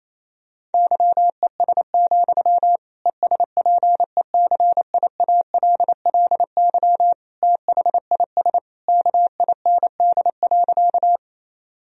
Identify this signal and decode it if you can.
Morse code, heard as a keyed tone, about 28 wpm, 700 Hz. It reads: YES, ESPECIALLY THIS KIND.
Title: Sentences from Top 1000 Words - Review Part 4 - 28wpm